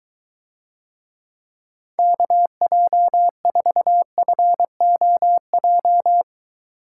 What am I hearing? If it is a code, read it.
Morse code, heard as a keyed tone, about 23 wpm, 700 Hz. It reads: KJ4FOJ